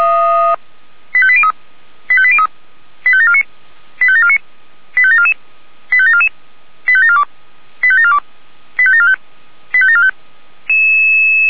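Five-tone selective call (ZVEI), heard as quick runs of five tones, tones of about 70 ms. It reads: 85803 85803 85749 85749 85744 85744 85742 85742 85745 85745
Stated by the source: ZVEI five-tone call received with an SDR.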